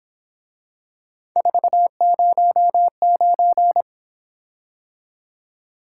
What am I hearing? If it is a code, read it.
Morse code, heard as a keyed tone, about 26 wpm, 700 Hz. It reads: 409
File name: Speed-Racing - Three Letter-Number Combinations - 22wpm